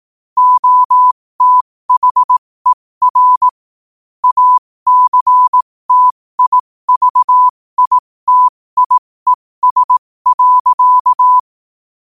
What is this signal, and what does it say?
Morse code, heard as a keyed tone, about 18 wpm, spaced as for 15 wpm, 1000 Hz. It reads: OTHER ACTIVITIES.
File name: qotdfiles / 15wpm